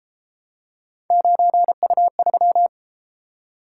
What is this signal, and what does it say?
Morse code, heard as a keyed tone, about 29 wpm, 700 Hz. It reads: 9U3